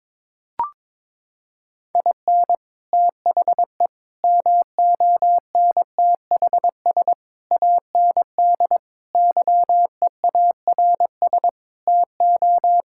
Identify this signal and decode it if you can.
Morse code, heard as a keyed tone, about 22 wpm, 700 Hz. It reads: IN THE MONTHS AND YEARS TO
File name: practice / 22wpm / Sets of 7 Words - 22wpm